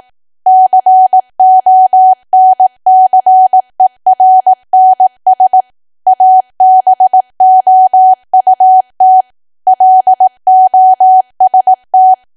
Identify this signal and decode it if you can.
Morse code, heard as a keyed tone, about 18 wpm, 750 Hz. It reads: CONCERNS ABOUT LOST
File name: cwa / W1AW-Level2 / 18WPM